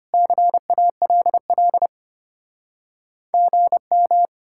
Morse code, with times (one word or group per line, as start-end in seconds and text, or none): 0.1-1.9 CALL
3.3-4.3 GM